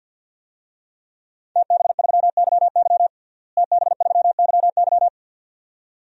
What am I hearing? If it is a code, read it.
Morse code, heard as a keyed tone, about 52 wpm, 700 Hz. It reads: TB3YY TB3YY
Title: DXCC-100 Call Signs - 5 Characters - Repeated 2x - 50wpm